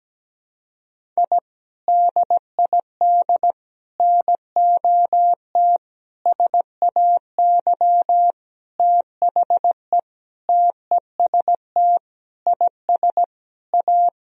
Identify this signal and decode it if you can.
Morse code, heard as a keyed tone, about 17 wpm, 700 Hz. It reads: I DID NOT SAY THE TEST IS A
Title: Sentences from Top 1000 Words - Review Part 3 - 17wpm